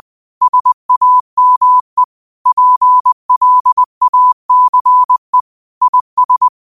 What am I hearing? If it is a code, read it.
Morse code, heard as a keyed tone, about 20 wpm, 1000 Hz. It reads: SAME PLACE IS
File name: qotdfiles / 20wpm